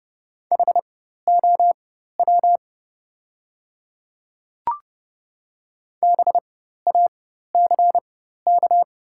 Morse code, heard as a keyed tone, about 30 wpm, spaced as for 14 wpm, 690 Hz. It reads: HOW BACK
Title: 4x Character Spacing - Top 200 Words - 30wpm